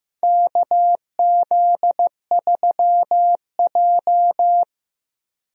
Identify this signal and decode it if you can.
Morse code, heard as a keyed tone, about 15 wpm, 700 Hz. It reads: KZ3J
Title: USA Call Signs - 4 Characters - Repeated 3x - 15wpm